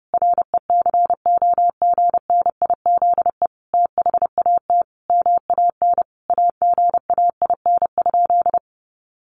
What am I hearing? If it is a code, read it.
Morse code, heard as a keyed tone, about 30 wpm, 700 Hz. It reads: RECOGNIZE THAT MAN AGAIN?
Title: Sentences from Top 1400 Words 30wpm